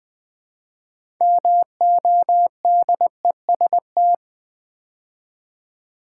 Morse code, heard as a keyed tone, about 20 wpm, 700 Hz. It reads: MODEST
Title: Build A Word - 2 Letters At A Time - 6-Letter Words - 20wpm